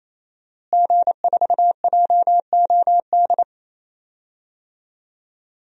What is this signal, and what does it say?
Morse code, heard as a keyed tone, about 28 wpm, 700 Hz. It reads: G4JOD
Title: DXCC-100 Call Signs - 5 Characters - Repeated 3x - 28wpm